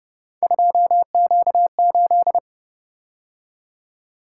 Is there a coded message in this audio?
Yes: Morse code from 0.4 to 2.4 s, heard as a keyed tone, about 30 wpm, 700 Hz. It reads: 2Q8